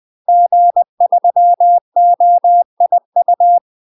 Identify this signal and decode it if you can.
Morse code, heard as a keyed tone, about 20 wpm, 700 Hz. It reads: G3OIU
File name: UK MASTER QSOs TRAINING / 20WPM / UK MASTER QSOs 0005